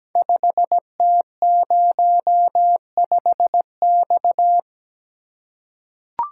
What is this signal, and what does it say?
Morse code, heard as a keyed tone, about 17 wpm, 700 Hz. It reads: 5T05X
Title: Code Groups - 5 Letter:Numbers - 17wpm